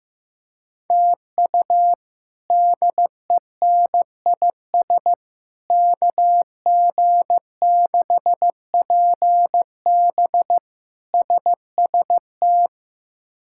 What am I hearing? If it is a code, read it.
Morse code, heard as a keyed tone, about 16 wpm, 700 Hz. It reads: TU DENIS KG6PB SST
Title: K1USN SST QSO Practice - 15wpm